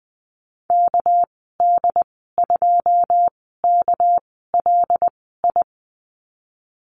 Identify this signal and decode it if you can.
Morse code, heard as a keyed tone, about 20 wpm, spaced as for 15 wpm, 700 Hz. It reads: KD2KLI
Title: 219 Lesson 35g - Number 0 - Call Signs 20wpm